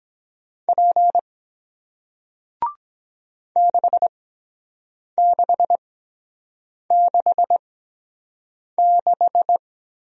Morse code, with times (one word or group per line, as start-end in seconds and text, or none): 0.7-1.2 P
3.6-9.6 6666